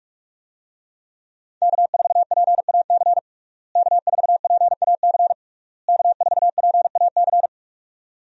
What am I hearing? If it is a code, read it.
Morse code, heard as a keyed tone, about 46 wpm, 700 Hz. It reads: K4PAC K4PAC K4PAC
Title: USA Call Signs - 5 Characters - Repeated 3x - 45wpm